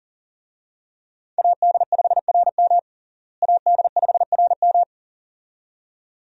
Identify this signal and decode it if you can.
Morse code, heard as a keyed tone, about 40 wpm, 700 Hz. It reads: AD5RM AD5RM
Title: USA Call Signs - 5 Characters - Repeated 2x - 40wpm